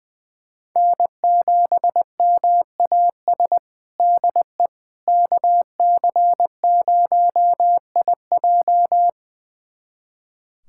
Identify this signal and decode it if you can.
Morse code, heard as a keyed tone, about 21 wpm, 700 Hz. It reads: N7MAS DE KC0IJ
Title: USA Callsign DE Callsign - 5 Characters - 20wpm